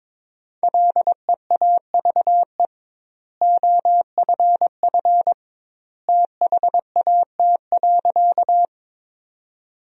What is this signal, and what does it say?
Morse code, heard as a keyed tone, about 22 wpm, 700 Hz. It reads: LEAVE OFF THAT.
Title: 2x Word-Spacing - Sentences from Top 400 Words - 22wpm